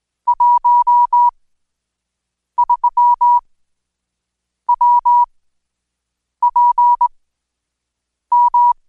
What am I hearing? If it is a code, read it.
Morse code, heard as a keyed tone, about 21 wpm, spaced as for 6 wpm, 970 Hz. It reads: 13 WPM
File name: Code Quick/AudioDownld2/Seg109-Day14-DayDrill 14